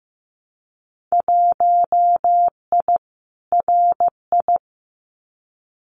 Morse code, heard as a keyed tone, about 15 wpm, 700 Hz. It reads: 1I RI
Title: Winter Field Day Exchanges Encore 15wpm